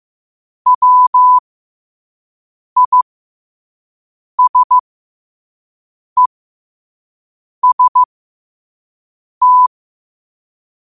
Morse code, heard as a keyed tone, about 15 wpm, spaced as for 5 wpm, 1000 Hz. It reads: WISEST
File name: qotdfiles / 5wpm